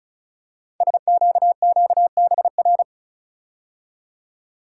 Morse code, heard as a keyed tone, about 36 wpm, 690 Hz. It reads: SQQBR
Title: Code Groups - 5 Letters - 35wpm